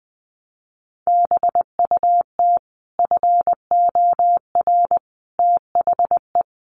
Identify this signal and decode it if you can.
Morse code, heard as a keyed tone, about 20 wpm, 700 Hz. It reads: BUT FOR THE
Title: Sets of 5 Words Encore 20wpm